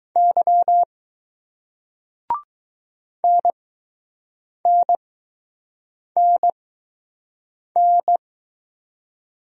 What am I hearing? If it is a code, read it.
Morse code, heard as a keyed tone, about 20 wpm, spaced as for 6 wpm, 700 Hz. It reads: Y NNNN